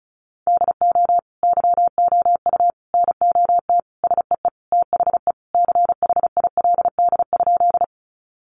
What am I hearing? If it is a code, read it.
Morse code, heard as a keyed tone, about 34 wpm, 700 Hz. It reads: DO YOU NOT SEE THE CHILD?
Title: Sentences from Top 800 Words 35wpm